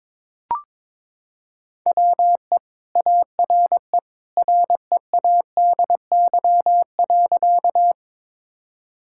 Morse code, heard as a keyed tone, about 22 wpm, 700 Hz. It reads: WE ARE READY.
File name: practice / 22wpm / Sentences with 2-Word Phrases - 22wpm